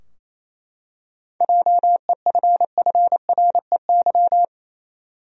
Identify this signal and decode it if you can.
Morse code, heard as a keyed tone, about 28 wpm, 700 Hz. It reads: JEFFREY
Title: USA Common Names - 28wpm